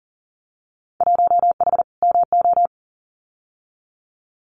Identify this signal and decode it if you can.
Morse code, heard as a keyed tone, about 42 wpm, 700 Hz. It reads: 1H MO
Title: Winter Field Day Exchanges 40wpm